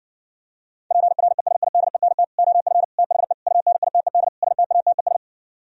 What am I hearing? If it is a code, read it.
Morse code, heard as a keyed tone, about 59 wpm, 700 Hz. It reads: PRESIDENT OF THE UNITED STATES